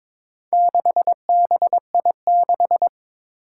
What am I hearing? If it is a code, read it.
Morse code, heard as a keyed tone, about 22 wpm, 700 Hz. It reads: 6BI6
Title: Four Challenging Letter-Number Combinations 22wpm